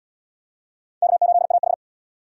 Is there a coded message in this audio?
Yes: Morse code from 1.0 to 1.7 s, heard as a keyed tone, about 67 wpm, 700 Hz. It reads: K8AH